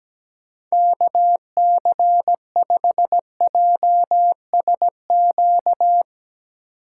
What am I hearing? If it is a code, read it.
Morse code, heard as a keyed tone, about 17 wpm, 700 Hz. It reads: KC5JSQ
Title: USA Call Signs - 6 Characters - 4x Character Spacing - 17wpm